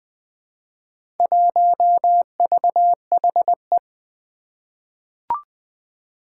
Morse code, heard as a keyed tone, about 20 wpm, 700 Hz. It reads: 1VHE
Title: Four Challenging Letter-Number Combinations 20wpm